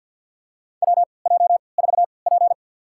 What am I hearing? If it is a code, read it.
Morse code, heard as a keyed tone, about 53 wpm, spaced as for 29 wpm, 700 Hz. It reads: WJ4P
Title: USA Call Signs - 4 Characters - 3x Character Spacing - 50wpm